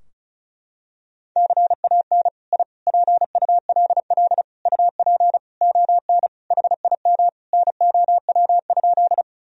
Morse code, heard as a keyed tone, about 36 wpm, 700 Hz. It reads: CAN I PULL UP ON HIM NOW?
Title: Sentences from Top 1000 Words - Review Part 2 - 35wpm